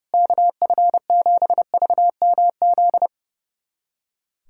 Morse code, heard as a keyed tone, about 30 wpm, 700 Hz. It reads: KF7VMZ
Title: USA Callsign DE Callsign - 6 Characters - 30wpm